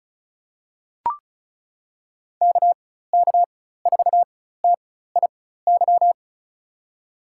Morse code, heard as a keyed tone, about 36 wpm, spaced as for 16 wpm, 700 Hz. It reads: KK4TIY
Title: USA Call Signs - 4x Character Spacing - 35wpm